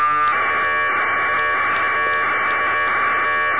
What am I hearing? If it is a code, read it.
Packet radio (AX.25, as in APRS).
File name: uploader